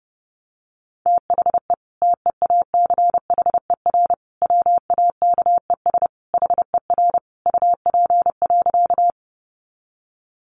Morse code, heard as a keyed tone, about 30 wpm, 700 Hz. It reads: THE TEACHER WAKES HER UP.
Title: Easy Phrases 30wpm